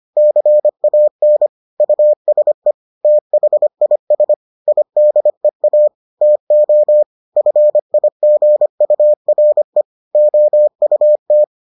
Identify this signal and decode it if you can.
Morse code, heard as a keyed tone, about 25 wpm, 600 Hz. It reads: CAN USE THIS IDEA TO FIGURE OUT